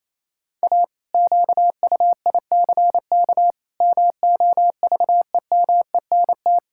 Morse code, heard as a keyed tone, about 27 wpm, 700 Hz. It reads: A QUICK MOVEMENT
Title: Pangram Challenge v2 - 28wpm